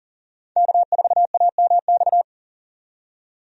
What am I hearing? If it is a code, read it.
Morse code, heard as a keyed tone, about 39 wpm, 700 Hz. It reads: K4AMX
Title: USA Call Signs - 5 Characters - 40wpm